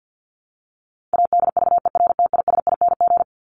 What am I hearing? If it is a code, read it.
Morse code, heard as a keyed tone, about 50 wpm, 700 Hz. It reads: ADVERTISING
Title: Top 751-1000 Words 50wpm